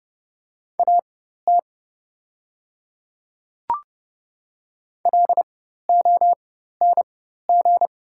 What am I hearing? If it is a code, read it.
Morse code, heard as a keyed tone, about 30 wpm, spaced as for 14 wpm, 700 Hz. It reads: AT LONG
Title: 4x Character Spacing - Top 100 Words - 30wpm